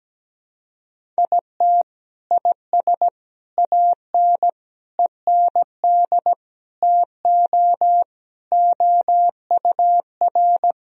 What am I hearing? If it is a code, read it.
Morse code, heard as a keyed tone, about 17 wpm, 700 Hz. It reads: IT IS AN END TO OUR
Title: Sentences from Top 1000 Words - Review Part 1 - 17wpm